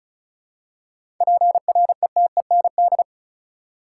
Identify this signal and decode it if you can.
Morse code, heard as a keyed tone, about 36 wpm, 700 Hz. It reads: PRETEND
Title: Seven Letter Words - 35wpm